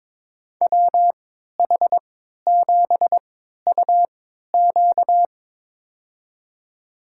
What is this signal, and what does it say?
Morse code, heard as a keyed tone, about 22 wpm, spaced as for 12 wpm, 700 Hz. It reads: WH7UQ
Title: USA Call Signs - 5 Characters - 3x Character Spacing - 22wpm